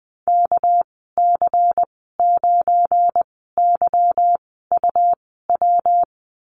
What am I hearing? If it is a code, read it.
Morse code, heard as a keyed tone, about 20 wpm, spaced as for 14 wpm, 700 Hz. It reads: KC9YUW
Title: 245 Lesson 39g - Letter z - Call Signs 20wpm